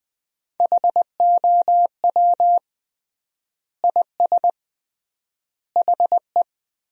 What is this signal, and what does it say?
Morse code, heard as a keyed tone, about 20 wpm, 700 Hz. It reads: HOW IS HE
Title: 3x Word-Spacing - Sentences from Top 100 Words - 20wpm